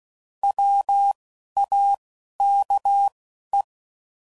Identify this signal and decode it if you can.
Morse code, heard as a keyed tone, about 16 wpm, 800 Hz. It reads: W A K E
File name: Prefixes / By First Letter / Ws